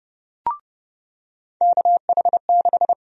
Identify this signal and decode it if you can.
Morse code, heard as a keyed tone, about 30 wpm, 700 Hz. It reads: KH6
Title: USA Call Sign Prefixes - 30wpm